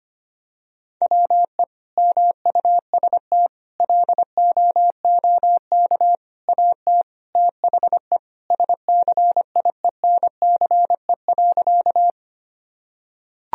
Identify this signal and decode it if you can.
Morse code, heard as a keyed tone, about 25 wpm, 700 Hz. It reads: WE MUST LOOK AT THE SCIENCE.
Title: Sentences from Top 1000 Words - Review Part 2 - 25wpm